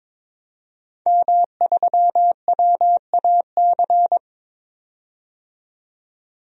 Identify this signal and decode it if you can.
Morse code, heard as a keyed tone, about 22 wpm, 700 Hz. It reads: M3WAC
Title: DXCC-100 Call Signs - 5 Characters - 22wpm